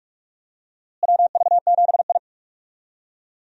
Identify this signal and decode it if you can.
Morse code, heard as a keyed tone, about 46 wpm, 700 Hz. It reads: WV7I
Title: USA Call Signs - 4 Characters - Repeated 3x - 45wpm